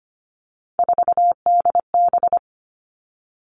Morse code, heard as a keyed tone, about 25 wpm, 700 Hz. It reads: 4DB
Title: All Cut Numbers - 3-Digit 25wpm